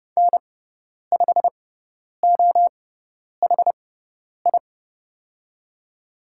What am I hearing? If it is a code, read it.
Morse code, heard as a keyed tone, about 30 wpm, spaced as for 10 wpm, 700 Hz. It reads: N5OHI
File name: Lesson 16g - Letter c - Call Signs 30wpm